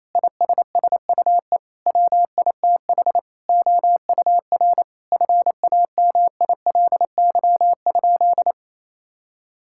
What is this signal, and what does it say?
Morse code, heard as a keyed tone, about 28 wpm, 700 Hz. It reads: ISSUE WITH OUR FAMILY?
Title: Sentences from Top 1000 Words - Review Part 6 - 28wpm